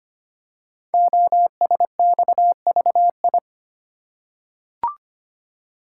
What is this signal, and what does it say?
Morse code, heard as a keyed tone, about 25 wpm, 700 Hz. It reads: OSXVI
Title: Code Groups - 5 Letters - 25wpm